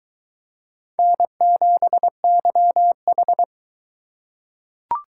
Morse code, heard as a keyed tone, about 21 wpm, 700 Hz. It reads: N7YH